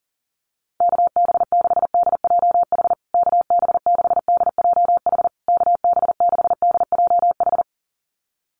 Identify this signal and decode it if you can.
Morse code, heard as a keyed tone, about 40 wpm, 700 Hz. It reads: KB6DJH KB6DJH KB6DJH
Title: USA Call Signs Repeated 3x 40wpm